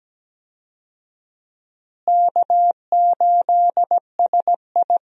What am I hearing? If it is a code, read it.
Morse code, heard as a keyed tone, about 17 wpm, 700 Hz. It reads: K8SI
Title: USA Call Signs - 4 Characters - Repeated 3x - 17wpm